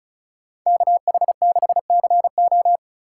Morse code, heard as a keyed tone, about 44 wpm, 700 Hz. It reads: KH6CO